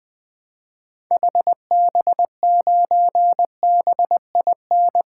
Morse code, heard as a keyed tone, about 20 wpm, 700 Hz. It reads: HB9BIN
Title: 2023 SOTA Activator Call Signs - 20wpm